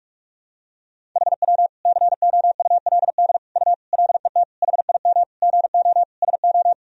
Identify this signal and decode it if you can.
Morse code, heard as a keyed tone, about 43 wpm, 700 Hz. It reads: HW COULD U LET HIM GO SO
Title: Sentences from Top 300 Words and Abbreviations 45wpm